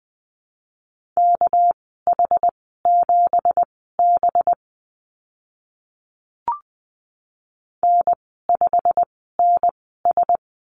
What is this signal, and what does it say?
Morse code, heard as a keyed tone, about 20 wpm, spaced as for 15 wpm, 700 Hz. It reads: KH7B N5NS
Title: 207 Lesson 33g - Letter j - Call Signs 20wpm